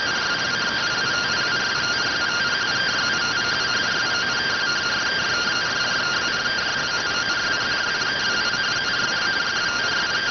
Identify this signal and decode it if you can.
Two-tone teleprinter signal, about 170 Hz shift.